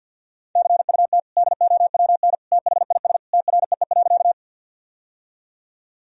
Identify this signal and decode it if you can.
Morse code, heard as a keyed tone, about 51 wpm, 700 Hz. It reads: CUT DOWN THIS TREE.
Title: Sentences from Top 1000 Words - Review Part 1 - 50wpm